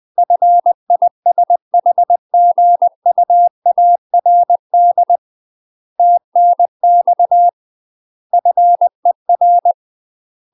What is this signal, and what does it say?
Morse code, heard as a keyed tone, about 20 wpm, 700 Hz. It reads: FISHGUARD TNX FER
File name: UK MASTER QSOs TRAINING / 20WPM / UK MASTER QSOs 0072